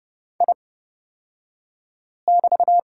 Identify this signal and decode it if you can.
Morse code, heard as a keyed tone, about 30 wpm, spaced as for 5 wpm, 700 Hz. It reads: I<BT>
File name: Lesson 36d - BT - Two Characters 30wpm